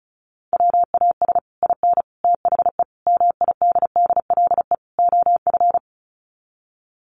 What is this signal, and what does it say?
Morse code, heard as a keyed tone, about 34 wpm, 700 Hz. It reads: WAS IN THE MIDDLE OF
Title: Sets of 5 Words Encore 35wpm